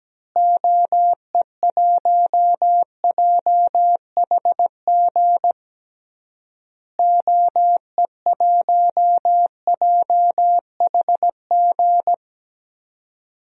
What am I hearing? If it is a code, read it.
Morse code, heard as a keyed tone, about 17 wpm, 700 Hz. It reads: OE1JHG OE1JHG
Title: DXCC-100 Call Signs - 6 Characters - Repeated 3x - 17wpm